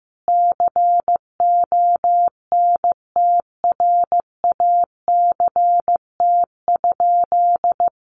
Morse code, heard as a keyed tone, about 15 wpm, 700 Hz. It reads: CONTRACT?
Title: Sentences from Top 1400 Words 15wpm